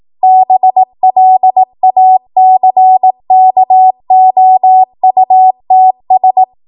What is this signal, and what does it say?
Morse code, heard as a keyed tone, about 18 wpm, 750 Hz. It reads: BLACKOUTS
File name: Morse / 18WPM